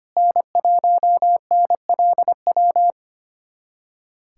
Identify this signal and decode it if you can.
Morse code, heard as a keyed tone, about 25 wpm, 700 Hz. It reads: N1NLW
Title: USA Callsign DE Callsign - 5 Characters - 25wpm